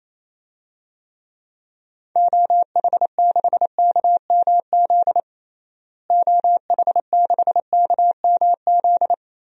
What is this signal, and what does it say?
Morse code, heard as a keyed tone, about 27 wpm, 700 Hz. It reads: OH6KMZ OH6KMZ
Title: DXCC-100 Call Signs - 6 Characters - Repeated 3x - 28wpm